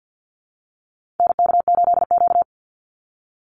Morse code, heard as a keyed tone, about 50 wpm, 700 Hz. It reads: NK8Q
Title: NA SOTA Activator Callsigns 50wpm